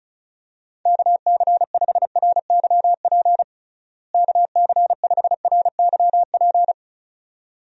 Morse code, heard as a keyed tone, about 34 wpm, 700 Hz. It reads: KC5RYP KC5RYP
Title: USA Call Signs - 6 Characters - Repeated 2x - 35wpm